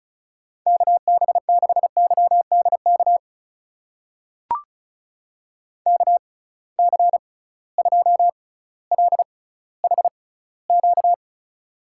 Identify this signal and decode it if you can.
Morse code, heard as a keyed tone, about 36 wpm, spaced as for 18 wpm, 700 Hz. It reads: KB6YDK KC2LHQ
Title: USA Call Signs - 6 Characters - 6x Character Spacing - 35wpm